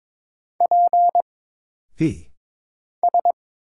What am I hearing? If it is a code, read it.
Morse code, heard as a keyed tone, about 22 wpm, 700 Hz. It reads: P S
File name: Single Character (Letter:Number:Punctuation:Prosign) - Mind-Melt - 22wpm